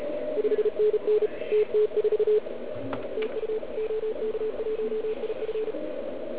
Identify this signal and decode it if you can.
Morse code, heard as a keyed tone, about 37 wpm, 440 Hz, with radio noise. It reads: 5NN TT4 TU OK1IF